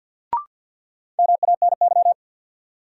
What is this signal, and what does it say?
Morse code, heard as a keyed tone, about 50 wpm, 700 Hz. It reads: MANY
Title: Sentences from Top 100 Words - New Word Drill - 50wpm